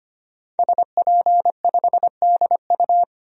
Morse code, heard as a keyed tone, about 25 wpm, 700 Hz. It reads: SP5DU
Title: DXCC-100 Call Signs - 5 Characters - 25wpm